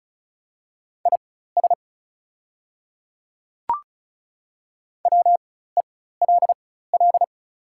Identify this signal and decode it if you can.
Morse code, heard as a keyed tone, about 34 wpm, spaced as for 16 wpm, 700 Hz. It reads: IS WELL
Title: 4x Character Spacing - Top 200 Words - 35wpm